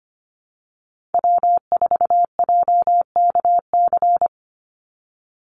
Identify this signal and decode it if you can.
Morse code, heard as a keyed tone, about 25 wpm, 700 Hz. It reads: W4JKC
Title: NA SOTA Chaser Callsigns 25wpm